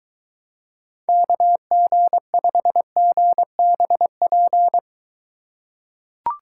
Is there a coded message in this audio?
Yes: Morse code from 1.1 to 4.8 s, heard as a keyed tone, about 23 wpm, 700 Hz. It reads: KG5GBP